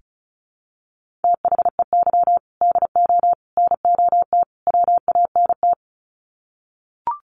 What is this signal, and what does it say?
Morse code, heard as a keyed tone, about 34 wpm, 700 Hz. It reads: THEY DO NOT WANT
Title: Sets of 4 Words 35wpm